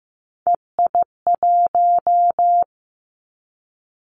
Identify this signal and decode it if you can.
Morse code, heard as a keyed tone, about 15 wpm, 700 Hz. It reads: EI1
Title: Callsign Prefixes for Top 50 DXCC - 15wpm